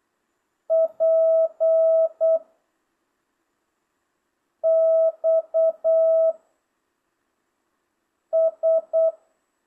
Morse code, heard as a keyed tone, about 8 wpm, 650 Hz. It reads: P X S